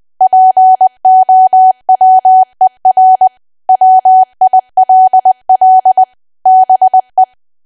Morse code, heard as a keyed tone, about 20 wpm, 750 Hz. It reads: POWER WILL BE
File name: cwa / W1AW-Level3 / 20WPM